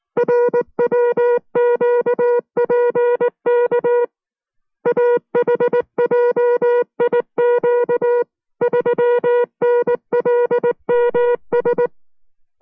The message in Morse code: RWQPK AHJIQ 3NLMS